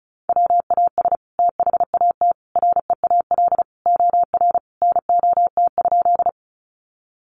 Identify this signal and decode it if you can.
Morse code, heard as a keyed tone, about 34 wpm, 700 Hz. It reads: WAS THAT REAL OR NOT?
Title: Sentences from Top 600 Words 35wpm